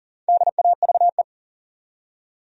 Morse code, heard as a keyed tone, about 40 wpm, 700 Hz. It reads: DAVE
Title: USA Common Names - 40wpm